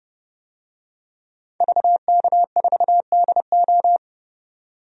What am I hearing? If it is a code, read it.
Morse code, heard as a keyed tone, about 30 wpm, 700 Hz. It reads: VK4DO
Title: DXCC-100 Call Signs - 5 Characters - 30wpm